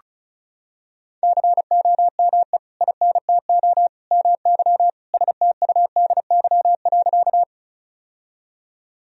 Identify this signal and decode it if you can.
Morse code, heard as a keyed tone, about 34 wpm, 700 Hz. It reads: COME INTO MY STUDY.
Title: Sentences from Top 1000 Words - Review Part 1 - 35wpm